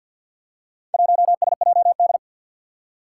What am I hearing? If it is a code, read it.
Morse code, heard as a keyed tone, about 48 wpm, 700 Hz. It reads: 1SJD
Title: Four Challenging Letter-Number Combinations 50wpm